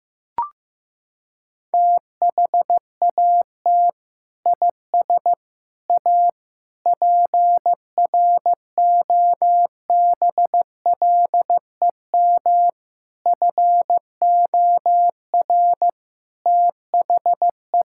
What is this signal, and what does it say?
Morse code, heard as a keyed tone, about 15 wpm, 700 Hz. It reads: THAT IS A PROBLEM FOR THE
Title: Sentences from Top 1600 Words 15wpm